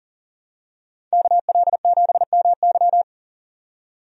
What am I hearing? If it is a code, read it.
Morse code, heard as a keyed tone, about 40 wpm, 700 Hz. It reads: KL7MY
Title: USA Call Signs - 5 Characters - 40wpm